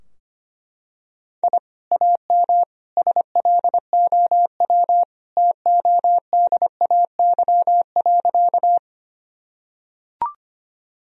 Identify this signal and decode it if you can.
Morse code, heard as a keyed tone, about 25 wpm, 700 Hz. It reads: I AM SLOW TODAY.
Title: Sentences from Top 1000 Words - Review Part 2 - 25wpm